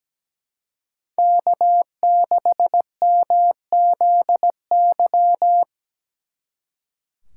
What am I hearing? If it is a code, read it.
Morse code, heard as a keyed tone, about 20 wpm, 700 Hz. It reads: K6MZY
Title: Speed Racing - USA 1x3 Call Signs 17wpm